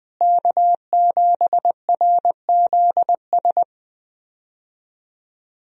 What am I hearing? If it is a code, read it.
Morse code, heard as a keyed tone, about 20 wpm, 700 Hz. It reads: K7RZS
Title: USA Call Signs - 5 Characters - Repeated 3x - 20wpm